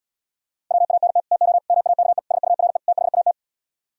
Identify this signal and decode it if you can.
Morse code, heard as a keyed tone, about 78 wpm, 700 Hz. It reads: WANT TO MAKE SURE THAT